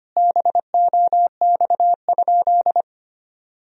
Morse code, heard as a keyed tone, about 25 wpm, 700 Hz. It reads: BOX?
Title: Sentences from Top 1000 Words - Review Part 4 - 25wpm